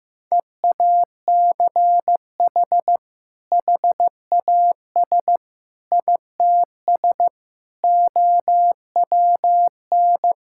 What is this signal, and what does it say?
Morse code, heard as a keyed tone, about 15 wpm, 700 Hz. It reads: EACH HAS ITS OWN
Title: Sentences from Top 1000 Words - Review Part 5 - 15wpm